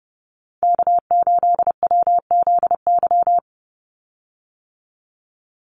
Morse code, heard as a keyed tone, about 30 wpm, 700 Hz. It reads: K8WZY
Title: USA Call Signs 30wpm